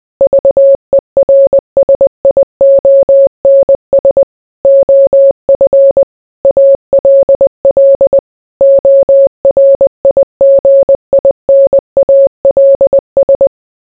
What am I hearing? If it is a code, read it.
Morse code, heard as a keyed tone, about 20 wpm, 550 Hz. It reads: VERSIONS OF ALL ORIGINALS